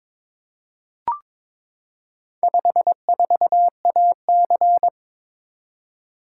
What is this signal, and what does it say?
Morse code, heard as a keyed tone, about 22 wpm, 700 Hz. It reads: 54AC
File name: practice / 22wpm / Four Letter-Number Combinations - 22wpm